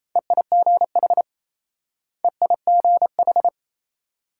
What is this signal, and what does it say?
Morse code, heard as a keyed tone, about 35 wpm, 700 Hz. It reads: EIGH EIGH
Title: Speed-Racing Multi-letter Phonemes - 28wpm